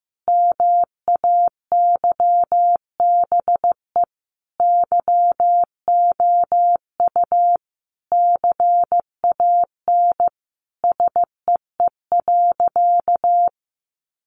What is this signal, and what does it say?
Morse code, heard as a keyed tone, about 15 wpm, 700 Hz. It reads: MAYBE YOU CAN SEE.
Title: Sentences from Top 1000 Words 15wpm